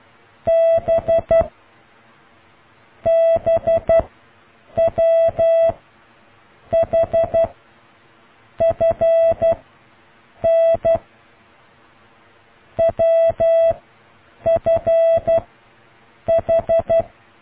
Morse code, heard as a keyed tone, about 12 wpm, 660 Hz. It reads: B B W H F N W F H